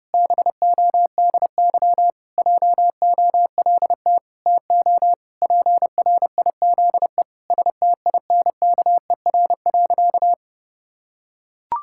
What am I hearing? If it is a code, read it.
Morse code, heard as a keyed tone, about 30 wpm, 700 Hz. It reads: BODY JOLT TO PRIZE STINKER.
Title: Pangram Challenge v2 - 30wpm